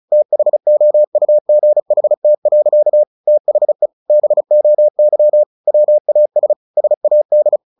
Morse code, heard as a keyed tone, about 34 wpm, 600 Hz. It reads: THOUGHT. THE BOY WAS SAD
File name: the old man and the sea / the old man and the sea 1 35wpm